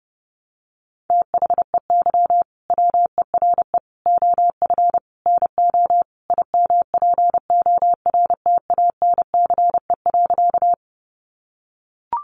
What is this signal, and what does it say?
Morse code, heard as a keyed tone, about 30 wpm, 700 Hz. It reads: THEY WERE OF NO IMPORTANCE.
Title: Sentences from Top 1200 Words 30wpm